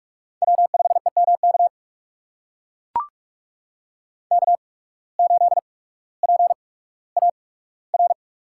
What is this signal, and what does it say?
Morse code, heard as a keyed tone, about 44 wpm, spaced as for 19 wpm, 700 Hz. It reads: W5EMK K8PAR
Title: USA Call Signs - 5 Characters - 8x Character Spacing - 45wpm